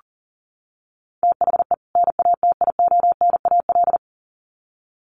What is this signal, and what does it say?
Morse code, heard as a keyed tone, about 40 wpm, 700 Hz. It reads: THE NATIONAL
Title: Sets of 2 Words 40wpm